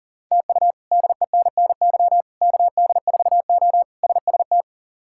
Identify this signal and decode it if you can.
Morse code, heard as a keyed tone, about 40 wpm, 690 Hz. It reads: TU DENNY KD4O SST
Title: K1USN SST QSO Practice - 40wpm